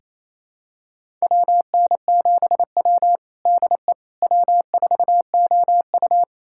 Morse code, heard as a keyed tone, about 27 wpm, 700 Hz. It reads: WN7W DE W4OU
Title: USA Callsign DE Callsign - 4 Characters - 28wpm